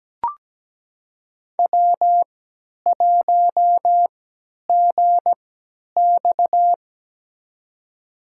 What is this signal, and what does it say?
Morse code, heard as a keyed tone, about 17 wpm, spaced as for 10 wpm, 700 Hz. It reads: W1GX
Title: USA Call Signs - 4 Characters - 3x Character Spacing - 17wpm